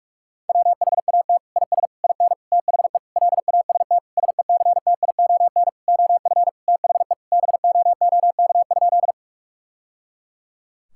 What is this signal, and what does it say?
Morse code, heard as a keyed tone, about 46 wpm, 700 Hz. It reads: WHAT IS IN THE LAST SECTION OF THE BOOK?
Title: Sentences from Top 1000 Words - Review Part 5 - 45wpm